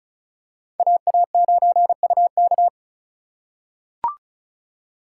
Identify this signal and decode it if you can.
Morse code, heard as a keyed tone, about 34 wpm, 700 Hz. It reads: AA9UK
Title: USA Call Signs - 3x Character Spacing - 35wpm